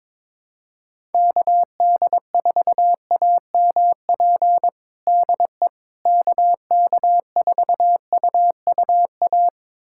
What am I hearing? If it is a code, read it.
Morse code, heard as a keyed tone, about 22 wpm, 700 Hz. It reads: KD4AMP DE KK4UUA
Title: USA Callsign DE Callsign 22wpm